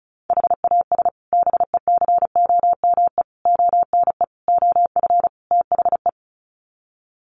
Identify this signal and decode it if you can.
Morse code, heard as a keyed tone, about 34 wpm, 700 Hz. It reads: HAS BECOME ONE OF THE
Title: Sets of 5 Words Encore 35wpm